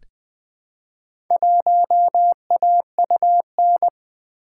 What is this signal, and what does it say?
Morse code, heard as a keyed tone, about 20 wpm, 700 Hz. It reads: 1AUN
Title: Four Letter-Number Combinations - 20wpm